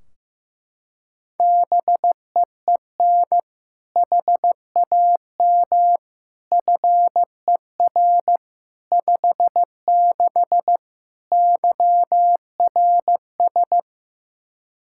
Morse code, heard as a keyed tone, about 15 wpm, 700 Hz. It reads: BEEN HAM FER 56 YRS
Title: Practice QSO - Exchange 3 - More Info - 15wpm